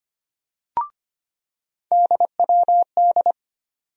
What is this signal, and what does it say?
Morse code, heard as a keyed tone, about 25 wpm, 700 Hz. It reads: DWD
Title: USA Call Sign Postfixes - 3 Letters 25wpm